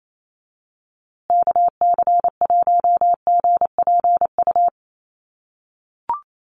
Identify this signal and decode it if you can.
Morse code, heard as a keyed tone, about 28 wpm, 700 Hz. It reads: KC1GPU
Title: USA Call Signs Repeated 2x 28wpm